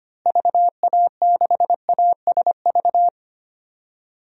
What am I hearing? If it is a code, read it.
Morse code, heard as a keyed tone, about 25 wpm, 700 Hz. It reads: VA6ASV
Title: DXCC-100 Call Signs - 6 Characters - Repeated 2x - 25wpm